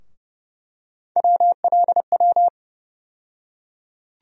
Morse code, heard as a keyed tone, about 31 wpm, 700 Hz. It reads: WLW
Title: USA Call Sign Postfixes - 3 Letters 30wpm